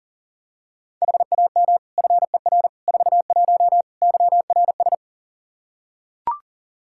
Morse code, heard as a keyed tone, about 40 wpm, 700 Hz. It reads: HAM FER 41 YRS
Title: K1USN SST QSO Practice 40wpm